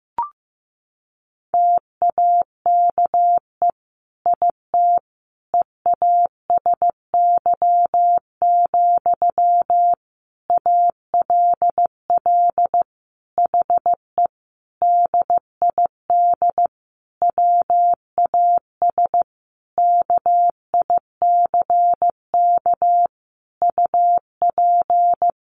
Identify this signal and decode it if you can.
Morse code, heard as a keyed tone, about 15 wpm, 700 Hz. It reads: TAKE IT EASY, ALL HE DID WAS KICK UP
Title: A Sound Of Thunder 15wpm